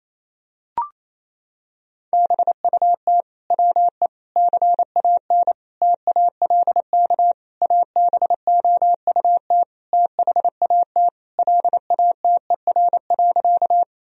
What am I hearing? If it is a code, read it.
Morse code, heard as a keyed tone, about 27 wpm, 700 Hz. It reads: BUT WE CAN TALK ABOUT THAT LATER.
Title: Sentences from Top 1000 Words - Review Part 4 - 28wpm